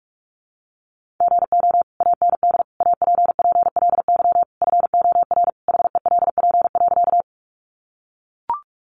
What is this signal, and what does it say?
Morse code, heard as a keyed tone, about 44 wpm, 700 Hz. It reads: GO AND APPLY FOR HELP.
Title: Sentences from Top 1200 Words 45wpm